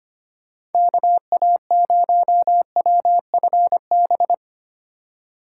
Morse code, heard as a keyed tone, about 34 wpm, 700 Hz. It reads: KA0WFB